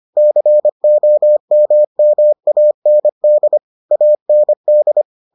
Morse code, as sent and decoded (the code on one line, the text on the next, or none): -.-. --- -- -- .- -. -.. / .- -. -..
COMMAND AND